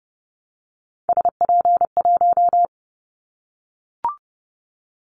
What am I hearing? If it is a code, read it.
Morse code, heard as a keyed tone, about 30 wpm, 700 Hz. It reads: SP1
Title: Callsign Prefixes for Top 25 DXCC - 30wpm